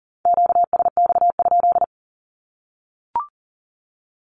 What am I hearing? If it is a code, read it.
Morse code, heard as a keyed tone, about 40 wpm, 700 Hz. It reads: QSX?
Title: Q-Codes 40wpm